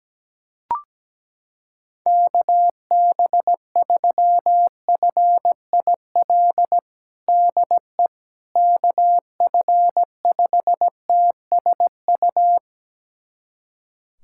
Morse code, heard as a keyed tone, about 17 wpm, 700 Hz. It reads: KB3FIL DE KF5TSU
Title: USA Callsign DE Callsign - 6 Characters - 17wpm